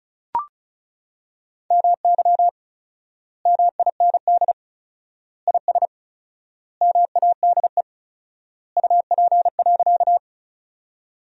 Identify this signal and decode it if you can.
Morse code, heard as a keyed tone, about 34 wpm, 700 Hz. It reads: MY MIND IS MADE UP.
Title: 4x Word-Spacing - Sentences from Top 500 Words - 35wpm